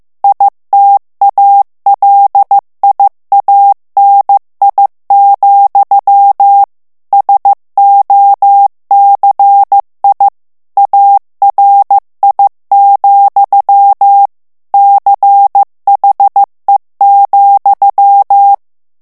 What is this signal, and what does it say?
Morse code, heard as a keyed tone, about 15 wpm, 800 Hz. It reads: ITALIANI, SOCI ARI, CHE,
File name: morse mp3 / ITALIEN texte clair 2 vitesse 14 mots minute